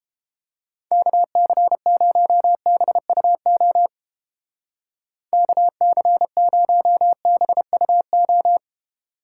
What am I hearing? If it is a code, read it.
Morse code, heard as a keyed tone, about 29 wpm, 700 Hz. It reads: KC0BUO KC0BUO